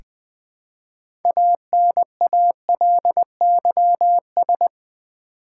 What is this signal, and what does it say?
Morse code, heard as a keyed tone, about 20 wpm, 700 Hz. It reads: ANALYS
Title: Build A Word - 3 Letters At A Time - 8-Letter Words - 20wpm